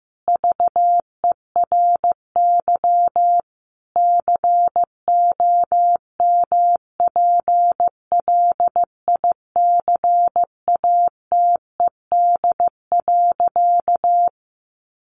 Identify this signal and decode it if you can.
Morse code, heard as a keyed tone, about 15 wpm, 700 Hz. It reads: VERY COMPLICATED.
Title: Sentences from Top 1200 Words 15wpm